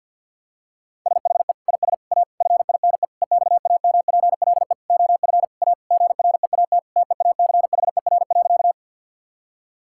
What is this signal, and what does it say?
Morse code, heard as a keyed tone, about 50 wpm, 700 Hz. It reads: SHE IS A FINE EXAMPLE OF A GREAT TEACHER.
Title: Sentences from Top 1000 Words - Review Part 6 - 50wpm